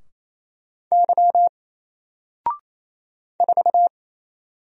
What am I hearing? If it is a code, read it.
Morse code, heard as a keyed tone, about 28 wpm, 700 Hz. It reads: Y 4
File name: Single-Letter-Number-28wpm